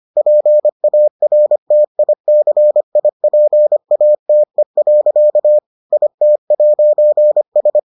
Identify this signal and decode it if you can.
Morse code, heard as a keyed tone, about 25 wpm, 600 Hz. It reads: PARTICIPATE. IT'S